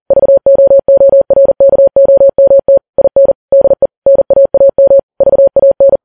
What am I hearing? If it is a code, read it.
Morse code, heard as a keyed tone, about 40 wpm, 550 Hz. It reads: VOORKOMT IN DE NAAM VAN
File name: Files Web tls / Morse on9fv / morse40kps